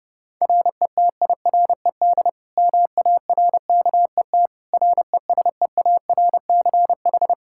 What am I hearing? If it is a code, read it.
Morse code, heard as a keyed tone, about 30 wpm, 700 Hz. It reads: RETIRED MARKET RESEARCH
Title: Practice QSO - Exchange 3 - More Info - 30wpm